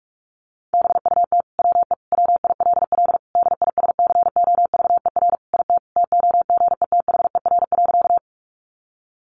Morse code, heard as a keyed tone, about 44 wpm, 700 Hz. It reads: BUT WE WILL DISCOVER IT TOGETHER.
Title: Sentences from Top 1100 Words 45wpm